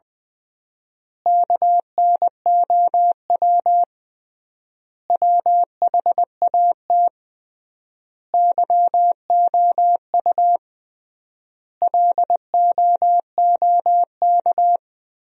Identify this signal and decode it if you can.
Morse code, heard as a keyed tone, about 20 wpm, 700 Hz. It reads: KNOW WHAT YOU LOOK
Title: 3x Word-Spacing - Sentences from Top 100 Words - 20wpm